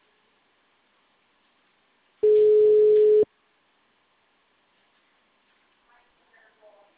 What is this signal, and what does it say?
Ringback tone (Europe, 425 Hz), with speech.
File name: out- -2006- - - .9581